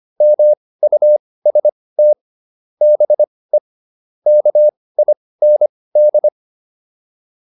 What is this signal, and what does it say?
Morse code, heard as a keyed tone, about 25 wpm, spaced as for 18 wpm, 600 Hz. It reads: MUST BE KIND
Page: intermediate-practice-files